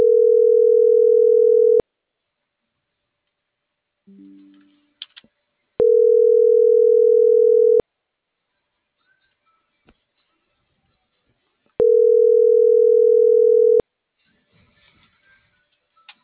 Ringback tone (North America), 440 + 480 Hz.